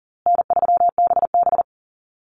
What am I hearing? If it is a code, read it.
Morse code, heard as a keyed tone, about 40 wpm, 700 Hz. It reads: N3BB
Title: Call Signs Repeated 3x - 40wpm